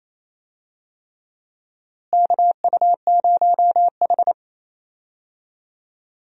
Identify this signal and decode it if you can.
Morse code, heard as a keyed tone, about 28 wpm, 700 Hz. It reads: KU0H
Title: USA Call Signs - 4 Characters - Repeated 3x - 28wpm